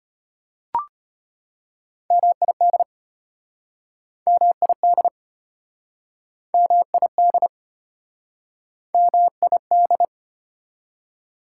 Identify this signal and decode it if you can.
Morse code, heard as a keyed tone, about 33 wpm, 690 Hz. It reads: MID MID MID MID